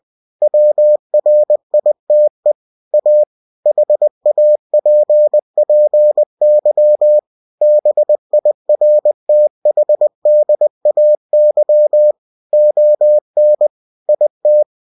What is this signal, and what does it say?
Morse code, heard as a keyed tone, about 20 wpm, 600 Hz. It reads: WRITE A HAPPY BIRTHDAY ON IT
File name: cw-books / winnie-the-pooh / winnie-20wpm-page-0051